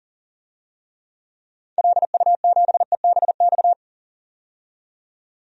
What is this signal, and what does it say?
Morse code, heard as a keyed tone, about 40 wpm, 700 Hz. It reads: LU7EBX
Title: DXCC-100 Call Signs - 6 Characters - 40wpm